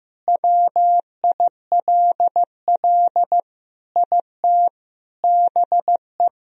Morse code, heard as a keyed tone, about 15 wpm, 700 Hz. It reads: WILL IT BE
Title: Sentences from Top 1000 Words - Review Part 3 - 15wpm